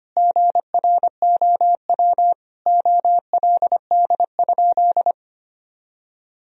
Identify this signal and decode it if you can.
Morse code, heard as a keyed tone, about 25 wpm, 700 Hz. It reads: GROW OLD?
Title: Sentences from Top 1000 Words - Review Part 1 - 25wpm